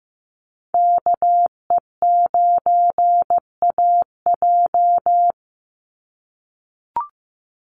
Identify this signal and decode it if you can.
Morse code, heard as a keyed tone, about 15 wpm, 700 Hz. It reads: KE9AJ
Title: NA SOTA Chaser Callsigns 15wpm